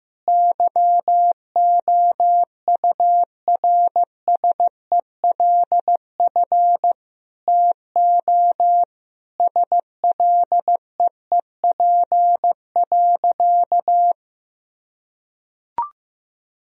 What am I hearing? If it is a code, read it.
Morse code, heard as a keyed tone, about 15 wpm, 700 Hz. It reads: YOURSELF TO SLEEP.
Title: Sentences from Top 1000 Words - Review Part 2 - 15wpm